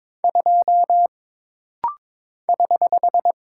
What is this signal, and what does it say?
Morse code, heard as a keyed tone, about 22 wpm, 700 Hz. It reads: 2 <HH>
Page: This rapid-fire timing is geared to push your ICR (Instant Character Recognition) skills.